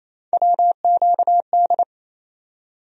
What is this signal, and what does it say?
Morse code, heard as a keyed tone, about 28 wpm, 700 Hz. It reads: WQD
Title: USA Call Sign Postfixes - 3 Letters 28wpm